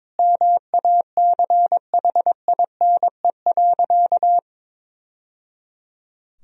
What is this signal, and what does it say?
Morse code, heard as a keyed tone, about 22 wpm, 700 Hz. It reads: MACHINE.
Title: Sentences from Top 1000 Words - Review Part 2 - 22wpm